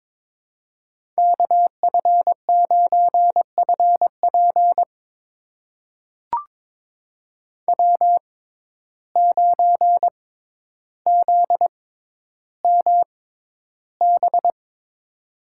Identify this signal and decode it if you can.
Morse code, heard as a keyed tone, about 22 wpm, spaced as for 11 wpm, 700 Hz. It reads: KF9FP W9ZMB
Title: USA Call Signs - 5 Characters - 6x Character Spacing - 22wpm